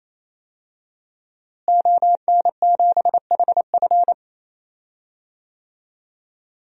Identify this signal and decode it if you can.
Morse code, heard as a keyed tone, about 28 wpm, 700 Hz. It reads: ON7HF
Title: DXCC-100 Call Signs - 5 Characters - Repeated 3x - 28wpm